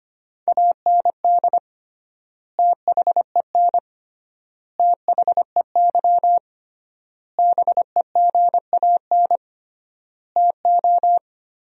Morse code, heard as a keyed tone, about 25 wpm, 700 Hz. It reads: AND THEN THEY BEGAN TO
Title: 3x Word-Spacing - Sentences from Top 300 Words - 25wpm